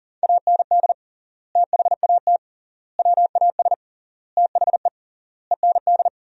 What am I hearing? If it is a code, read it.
Morse code, heard as a keyed tone, about 40 wpm, 700 Hz. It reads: AND THAT WAS THE END
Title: 3x Word-Spacing - Sentences from Top 200 Words - 40wpm